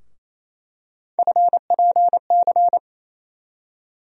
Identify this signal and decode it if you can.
Morse code, heard as a keyed tone, about 28 wpm, 700 Hz. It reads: FPC
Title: USA Call Sign Postfixes - 3 Letters 28wpm